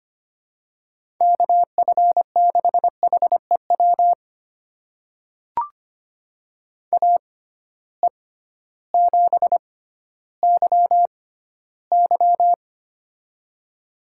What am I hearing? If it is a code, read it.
Morse code, heard as a keyed tone, about 25 wpm, spaced as for 14 wpm, 700 Hz. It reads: KF6HEW AE7YY
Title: USA Call Signs - 6x Character Spacing - 25wpm